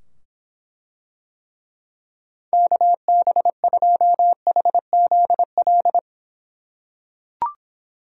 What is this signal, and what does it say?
Morse code, heard as a keyed tone, about 24 wpm, 700 Hz. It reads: KB2HZL